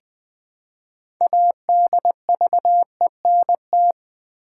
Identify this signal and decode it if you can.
Morse code, heard as a keyed tone, about 20 wpm, 700 Hz. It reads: ADVENT
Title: Build A Word - 3 Letters At A Time - 9-Letter Words - 20wpm